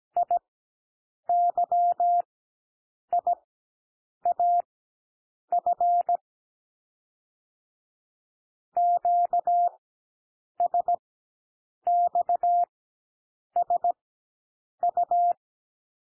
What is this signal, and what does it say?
Morse code, heard as a keyed tone, about 17 wpm, spaced as for 7 wpm, 700 Hz. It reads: IYIAF QSXSU
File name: SOUND LETTERS 17 wpm